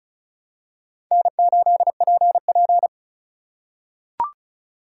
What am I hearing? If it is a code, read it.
Morse code, heard as a keyed tone, about 36 wpm, 700 Hz. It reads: N8PP
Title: USA Call Signs - 4 Characters - 4x Character Spacing - 35wpm